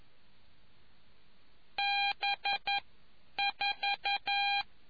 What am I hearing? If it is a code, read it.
Morse code, heard as a keyed tone, about 11 wpm, 800 Hz. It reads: B 4